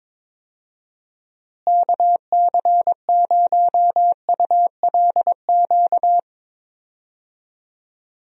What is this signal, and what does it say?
Morse code, heard as a keyed tone, about 22 wpm, 700 Hz. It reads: KC0ULQ
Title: USA Call Signs - 6 Characters - Repeated 3x - 22wpm